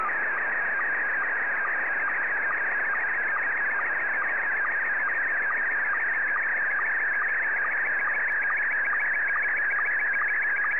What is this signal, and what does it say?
SSTV (slow-scan television), Scottie 1.